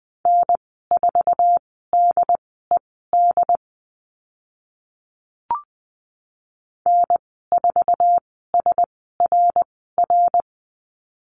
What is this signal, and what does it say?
Morse code, heard as a keyed tone, about 20 wpm, spaced as for 14 wpm, 700 Hz. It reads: N4DED N4SRR
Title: 076 Lesson 13g - Letter L - Call Signs 20wpm